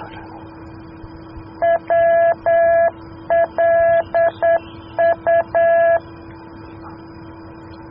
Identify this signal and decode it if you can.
Morse code, heard as a keyed tone, about 9 wpm, 670 Hz. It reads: WLU